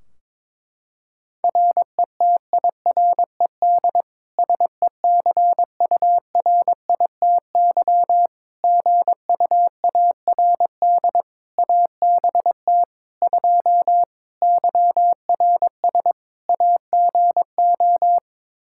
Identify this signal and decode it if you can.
Morse code, heard as a keyed tone, about 22 wpm, 700 Hz. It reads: RETIRED SECURITY GUARD ABT 2 YRS AGO
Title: Practice QSO - Exchange 3 - More Info - 22wpm